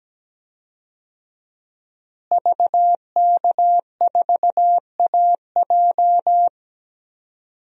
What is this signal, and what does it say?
Morse code, heard as a keyed tone, about 17 wpm, 700 Hz. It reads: VK4AJ
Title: DXCC-100 Call Signs - 5 Characters - Repeated 2x - 17wpm